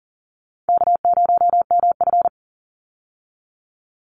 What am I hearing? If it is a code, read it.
Morse code, heard as a keyed tone, about 40 wpm, 700 Hz. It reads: K0MF
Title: USA Call Signs 40wpm